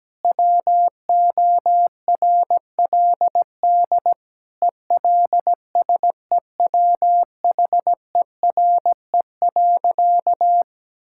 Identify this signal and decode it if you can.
Morse code, heard as a keyed tone, about 17 wpm, 700 Hz. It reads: WORLD ELSEWHERE.
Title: Sentences from Top 1600 Words 17wpm